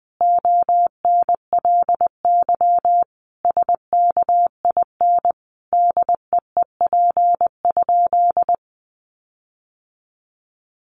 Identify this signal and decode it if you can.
Morse code, heard as a keyed tone, about 20 wpm, 700 Hz. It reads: ONLY SKIN DEEP?
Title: Sentences from Top 1300 Words 20wpm